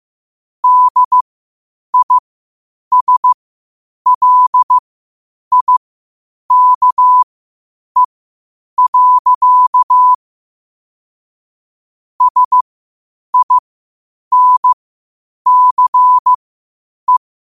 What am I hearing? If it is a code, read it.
Morse code, heard as a keyed tone, about 15 wpm, spaced as for 8 wpm, 1000 Hz. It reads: DISLIKE. SINCE
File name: qotdfiles / 8wpm